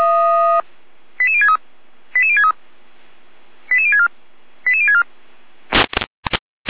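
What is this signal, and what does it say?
Five-tone selective call (ZVEI), heard as quick runs of five tones, tones of about 70 ms. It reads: 88063 88063 88064 88064
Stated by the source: ZVEI five-tone selective call received by SDR, from the Feuerwehr Orlamünde audio server.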